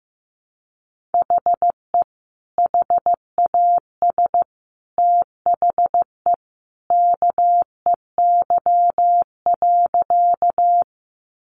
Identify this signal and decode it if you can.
Morse code, heard as a keyed tone, about 15 wpm, 700 Hz. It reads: HE HAS THE KEY.
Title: Sentences from Top 600 Words 15wpm